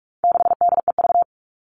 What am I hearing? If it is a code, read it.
Morse code, heard as a keyed tone, about 44 wpm, 700 Hz. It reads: 6DEV
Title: All Cut Numbers - 4-Digit 45wpm